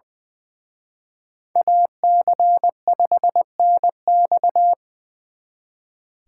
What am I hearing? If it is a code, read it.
Morse code, heard as a keyed tone, about 24 wpm, 700 Hz. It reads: AC5NX
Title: Speed Racing - USA 2x2 Call Signs 20wpm